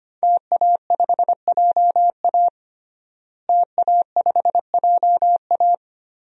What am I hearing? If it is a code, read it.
Morse code, heard as a keyed tone, about 25 wpm, 700 Hz. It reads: TA5JA TA5JA
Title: DXCC-100 Call Signs - 5 Characters - Repeated 2x - 25wpm